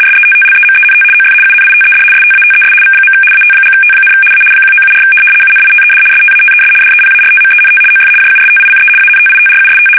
RTTY, about 75 baud. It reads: SETTLEMENT OF UNITED STATES V. AT&T, A 1974 UNITED STATES DEPARTMENT OF JUSTICE ANTITRUST
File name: fsk850-75B / teletype-corp